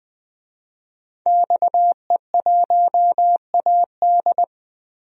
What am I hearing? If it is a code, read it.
Morse code, heard as a keyed tone, about 20 wpm, 700 Hz. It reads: XE1AD
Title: DXCC-100 Call Signs - 5 Characters - Repeated 2x - 20wpm